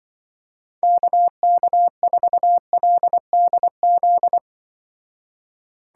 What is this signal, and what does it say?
Morse code, heard as a keyed tone, about 24 wpm, 700 Hz. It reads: KK4LDZ
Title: Speed Racing - USA 2x3 Call Signs 20wpm